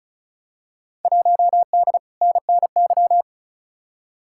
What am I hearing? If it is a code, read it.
Morse code, heard as a keyed tone, about 34 wpm, 700 Hz. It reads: 1D NNY
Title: Field Day Exchanges - 35wpm